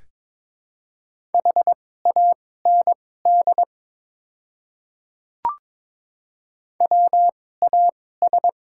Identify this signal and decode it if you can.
Morse code, heard as a keyed tone, about 21 wpm, spaced as for 16 wpm, 700 Hz. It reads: HAND WAS
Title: 2x Character Spacing - Top 500 Words - 22wpm